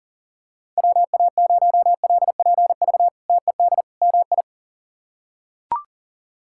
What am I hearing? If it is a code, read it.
Morse code, heard as a keyed tone, about 40 wpm, 700 Hz. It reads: WA0LPV TED MI
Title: K1USN SST QSO Practice - 40wpm